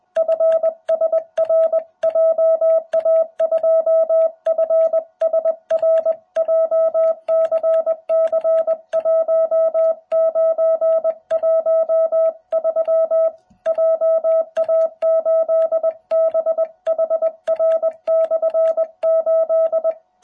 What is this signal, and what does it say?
Morse code, heard as a keyed tone, about 20 wpm, 650 Hz. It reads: FSR JA2FSR JCC1913 JA8BHR/8